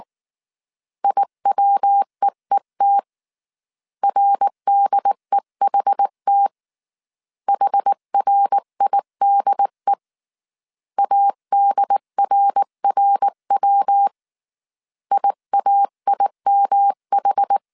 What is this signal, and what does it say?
Morse code, heard as a keyed tone, about 19 wpm, 800 Hz. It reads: IWEET RDEHT HRIDE ADRRW IAIMH
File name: CW / DET / 1 3 1500C S B 18wpm 1916